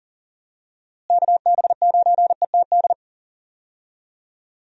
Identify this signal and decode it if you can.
Morse code, heard as a keyed tone, about 40 wpm, 700 Hz. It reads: KB9ETD
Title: USA Call Signs - 6 Characters - Repeated 3x - 40wpm